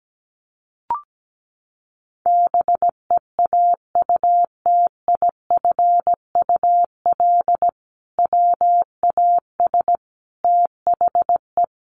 Morse code, heard as a keyed tone, about 17 wpm, 700 Hz. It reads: BEAUTIFUL WAS THE
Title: Sentences from Top 1100 Words 17wpm